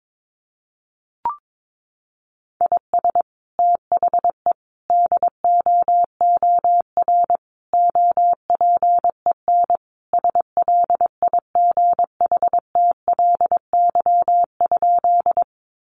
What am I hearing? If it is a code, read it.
Morse code, heard as a keyed tone, about 22 wpm, 700 Hz. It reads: IS THE DOOR OPEN SLIGHTLY?
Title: Sentences from Top 1300 Words 22wpm